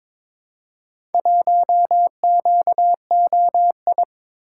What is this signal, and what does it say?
Morse code, heard as a keyed tone, about 22 wpm, 700 Hz. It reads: 1QOI
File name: practice / 22wpm / Four Letter-Number Combinations - 22wpm